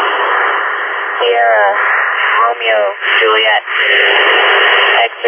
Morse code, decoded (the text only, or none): none